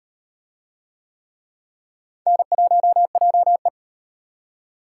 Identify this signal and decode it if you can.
Morse code, heard as a keyed tone, about 36 wpm, 700 Hz. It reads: N1JE